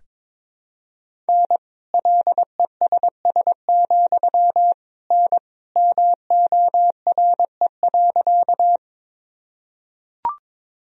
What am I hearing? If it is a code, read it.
Morse code, heard as a keyed tone, about 22 wpm, 700 Hz. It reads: N LESS, N MORE.
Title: Sentences from Top 400 Words and Abbreviations 22wpm